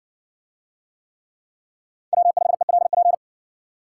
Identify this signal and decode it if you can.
Morse code, heard as a keyed tone, about 60 wpm, 700 Hz. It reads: W5ELP